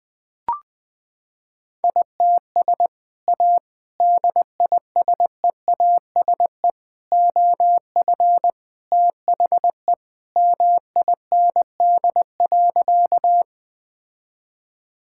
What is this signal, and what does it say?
Morse code, heard as a keyed tone, about 20 wpm, 700 Hz. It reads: ITS A DISEASE OF THE MIND.
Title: Sentences from Top 1000 Words - Review Part 6 - 20wpm